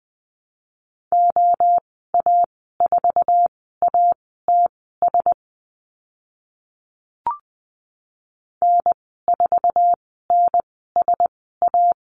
Morse code, heard as a keyed tone, about 20 wpm, spaced as for 14 wpm, 700 Hz. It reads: OA4ATS N4NSA
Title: 062 Lesson 11g - Letter h - Call Signs 20wpm